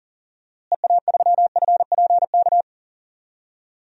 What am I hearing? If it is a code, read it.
Morse code, heard as a keyed tone, about 38 wpm, 700 Hz. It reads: EA3FPK
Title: DXCC-100 Call Signs - 6 Characters - Repeated 2x - 40wpm